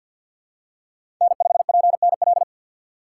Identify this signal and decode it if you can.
Morse code, heard as a keyed tone, about 47 wpm, 700 Hz. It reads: N5PNL
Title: USA Call Signs - 5 Characters - 6x Character Spacing - 50wpm